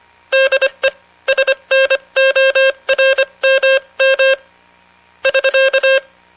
Morse code, heard as a keyed tone, about 25 wpm, spaced as for 22 wpm, 540 Hz. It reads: DE SNORMM <SK>